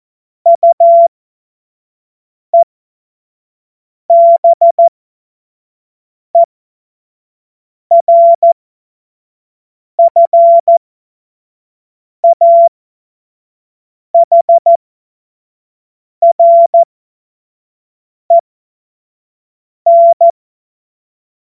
Morse code, heard as a keyed tone, about 14 wpm, spaced as for 5 wpm, 680 Hz. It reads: UEBERFAHREN
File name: Morsekurs / Witze / Witz 19. April 2022 25 F70